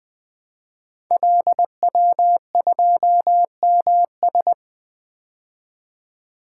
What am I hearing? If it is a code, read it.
Morse code, heard as a keyed tone, about 20 wpm, 700 Hz. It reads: LW2MS
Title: Code Groups - 5 Letter:Numbers - 20wpm